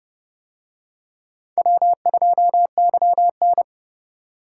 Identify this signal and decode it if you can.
Morse code, heard as a keyed tone, about 28 wpm, 700 Hz. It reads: W2YN